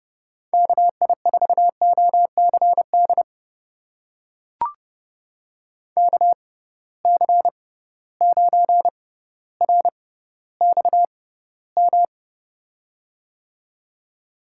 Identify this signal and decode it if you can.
Morse code, heard as a keyed tone, about 32 wpm, spaced as for 16 wpm, 700 Hz. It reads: KI4OCD KC9RXM
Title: USA Call Signs - 6 Characters - 6x Character Spacing - 30wpm